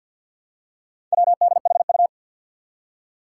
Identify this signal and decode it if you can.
Morse code, heard as a keyed tone, about 52 wpm, 700 Hz. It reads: WDHU
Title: Four Challenging Letter-Number Combinations 50wpm